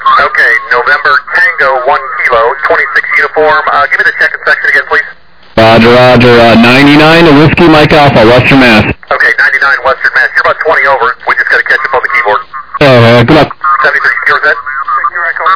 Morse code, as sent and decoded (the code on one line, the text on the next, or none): none